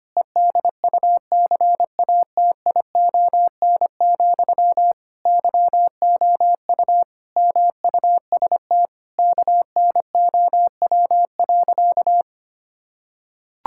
Morse code, heard as a keyed tone, about 25 wpm, 700 Hz. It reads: EDUCATION, YOU MUST KNOW.
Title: Sentences from Top 1000 Words - Review Part 6 - 25wpm